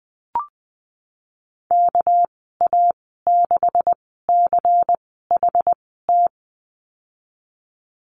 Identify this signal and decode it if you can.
Morse code, heard as a keyed tone, about 20 wpm, spaced as for 15 wpm, 700 Hz. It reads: KA6CHT
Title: 231 Lesson 37g - Letter x - Call Signs 20wpm